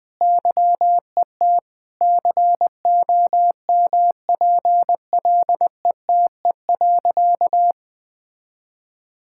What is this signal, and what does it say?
Morse code, heard as a keyed tone, about 20 wpm, 700 Hz. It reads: YET COMPLETE.
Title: Sentences from Top 1000 Words - Review Part 6 - 20wpm